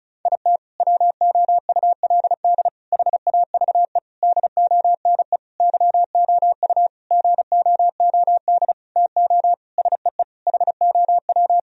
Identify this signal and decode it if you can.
Morse code, heard as a keyed tone, about 34 wpm, 700 Hz. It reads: IT WOULD HAVE DONE YOU GOOD TO SEE HOW
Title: Sentences from Top 1000 Words - Review Part 2 - 35wpm